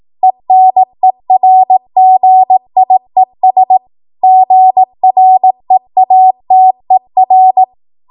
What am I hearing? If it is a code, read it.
Morse code, heard as a keyed tone, about 18 wpm, 750 Hz. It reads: ENERGIES GREATER
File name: Morse / 18WPM